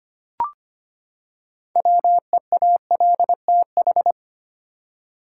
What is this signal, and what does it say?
Morse code, heard as a keyed tone, about 25 wpm, 700 Hz. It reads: WEALTH
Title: Six Letter Words - 25wpm